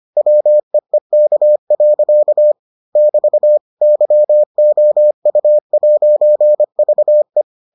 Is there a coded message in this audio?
Yes: Morse code from 0.2 to 7.4 s, heard as a keyed tone, about 25 wpm, 600 Hz. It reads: WEEK. <BT> YOU'VE